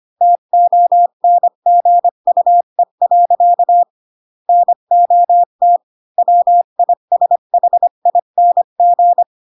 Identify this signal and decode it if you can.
Morse code, heard as a keyed tone, about 25 wpm, 700 Hz. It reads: TONGUE. NOT WISHING
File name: Pinocchio / Pinocchio ch3 20 25 ik1hge